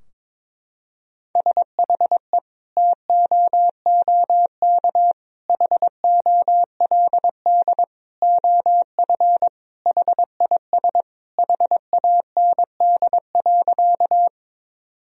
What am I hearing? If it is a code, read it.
Morse code, heard as a keyed tone, about 22 wpm, 700 Hz. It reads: SHE TOOK HOLD OF HIS HAND.
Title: Sentences from Top 1000 Words - Review Part 2 - 22wpm